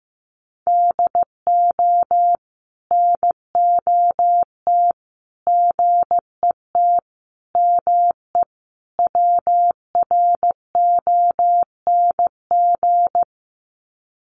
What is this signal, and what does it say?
Morse code, heard as a keyed tone, about 15 wpm, 700 Hz. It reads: DO NOT GET ME WRONG
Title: Sets of 5 Words Encore 15wpm